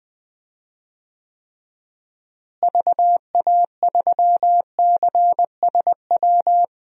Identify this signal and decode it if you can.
Morse code, heard as a keyed tone, about 20 wpm, 700 Hz. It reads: VA3CSW
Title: DXCC-100 Call Signs - 6 Characters - Repeated 2x - 20wpm